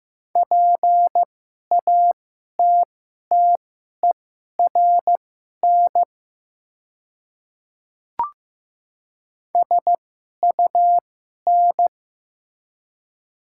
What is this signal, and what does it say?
Morse code, heard as a keyed tone, about 15 wpm, spaced as for 11 wpm, 700 Hz. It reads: PATTERN SUN
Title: 2x Character Spacing - Top 500 Words - 15wpm